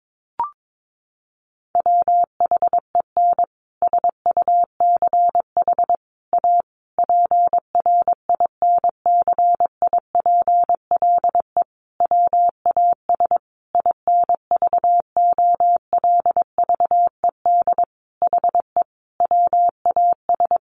Morse code, heard as a keyed tone, about 22 wpm, 700 Hz. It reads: WHEN SUCH A PRINCIPLE WAS INVOLVED HE WAS
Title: Sentences from Top 1400 Words 22wpm